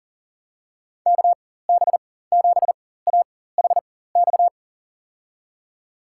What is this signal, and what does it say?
Morse code, heard as a keyed tone, about 40 wpm, spaced as for 19 wpm, 700 Hz. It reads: KB7AHX
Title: USA Call Signs - 6 Characters - 4x Character Spacing - 40wpm